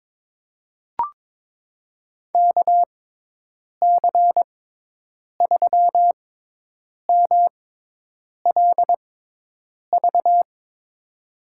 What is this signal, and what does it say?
Morse code, heard as a keyed tone, about 22 wpm, spaced as for 8 wpm, 700 Hz. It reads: KC3MLV
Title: USA Call Signs - 6 Characters - 6x Character Spacing - 22wpm